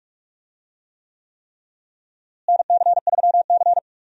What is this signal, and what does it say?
Morse code, heard as a keyed tone, about 42 wpm, 700 Hz. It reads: NC3C
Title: USA Call Signs - 4 Characters - Repeated 3x - 45wpm